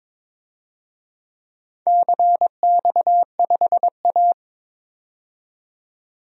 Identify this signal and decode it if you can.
Morse code, heard as a keyed tone, about 22 wpm, 700 Hz. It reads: CX5A
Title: DXCC-100 Call Signs - 4 Characters - 22wpm